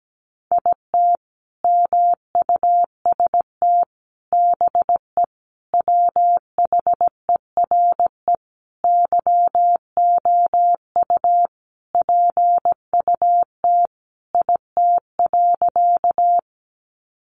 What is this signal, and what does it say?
Morse code, heard as a keyed tone, about 17 wpm, 700 Hz. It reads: IT MUST BE WHERE YOU PUT IT.
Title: Sentences from Top 200 Words 17wpm